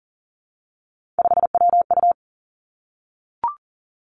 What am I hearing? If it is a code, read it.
Morse code, heard as a keyed tone, about 40 wpm, 700 Hz. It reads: 5WU
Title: Three Challenging Letter-Number Combinations 40wpm